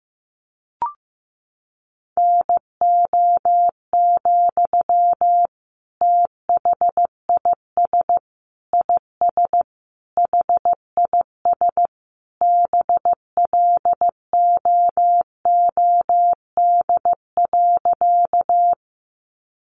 Morse code, heard as a keyed tone, about 15 wpm, 700 Hz. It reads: NO, THIS IS HIS BLOOD.
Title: Sentences from Top 1100 Words 15wpm